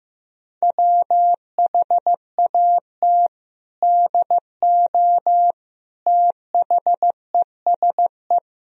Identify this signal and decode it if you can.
Morse code, heard as a keyed tone, about 15 wpm, 700 Hz. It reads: WHAT DO THESE
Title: Sentences from Top 1000 Words - Review Part 1 - 15wpm